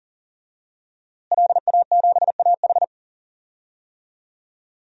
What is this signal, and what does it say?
Morse code, heard as a keyed tone, about 40 wpm, 700 Hz. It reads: LA7AH
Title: DXCC-100 Call Signs - 5 Characters - Repeated 2x - 40wpm